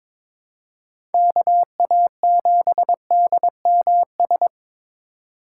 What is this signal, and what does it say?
Morse code, heard as a keyed tone, about 22 wpm, 700 Hz. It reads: KA7DMS
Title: USA Call Signs - 6 Characters - 3x Character Spacing - 22wpm